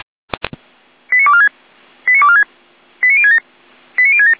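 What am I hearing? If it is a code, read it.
Five-tone selective call (ZVEI), heard as quick runs of five tones, tones of about 70 ms. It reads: 89246 89246 89067 89067
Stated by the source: Received by SDR.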